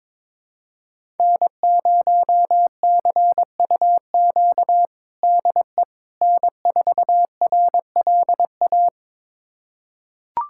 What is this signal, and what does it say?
Morse code, heard as a keyed tone, about 22 wpm, 700 Hz. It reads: N0CUQ DE N4RLA
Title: USA Callsign DE Callsign - 5 Characters - 22wpm